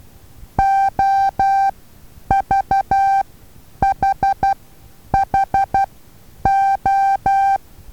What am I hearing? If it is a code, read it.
Morse code, heard as a keyed tone, about 12 wpm, spaced as for 9 wpm, 780 Hz. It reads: OVHHO
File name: Taviro HA6VA A